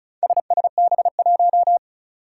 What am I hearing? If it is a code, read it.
Morse code, heard as a keyed tone, about 36 wpm, 700 Hz. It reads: SSB1
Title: Four Challenging Letter-Number Combinations 35wpm